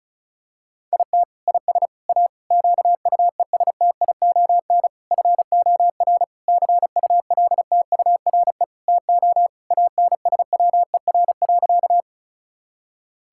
Morse code, heard as a keyed tone, about 36 wpm, 700 Hz. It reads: IT IS A QUESTION FOR CULTURE TO ANSWER.
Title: Sentences from Top 1000 Words - Review Part 6 - 35wpm